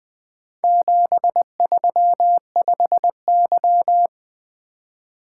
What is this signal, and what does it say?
Morse code, heard as a keyed tone, about 20 wpm, 700 Hz. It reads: 735Y
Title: Four Letter-Number Combinations - 20wpm